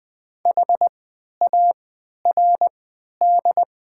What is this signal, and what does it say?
Morse code, heard as a keyed tone, about 20 wpm, spaced as for 11 wpm, 700 Hz. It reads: HARD
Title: 3x Character Spacing - Top 500 Words - 20wpm